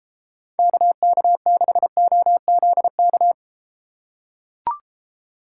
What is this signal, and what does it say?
Morse code, heard as a keyed tone, about 31 wpm, 700 Hz. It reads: KK6OZK